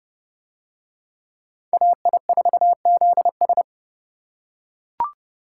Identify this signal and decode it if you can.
Morse code, heard as a keyed tone, about 30 wpm, 700 Hz. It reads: AI4ZS